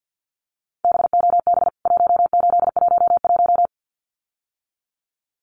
Morse code, heard as a keyed tone, about 49 wpm, 700 Hz. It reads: BOB 1811
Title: CWT Exchange 50wpm